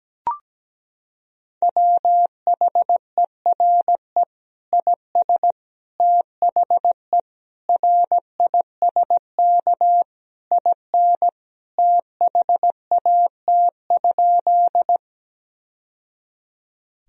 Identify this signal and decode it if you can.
Morse code, heard as a keyed tone, about 17 wpm, 700 Hz. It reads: WHERE IS THE RISK IN THAT?
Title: Sentences from Top 1000 Words - Review Part 4 - 17wpm